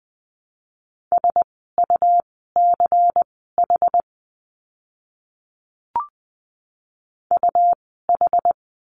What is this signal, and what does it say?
Morse code, heard as a keyed tone, about 20 wpm, spaced as for 14 wpm, 700 Hz. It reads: SUCH UH
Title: 101 Lesson 17f - Letter u - Words 20wpm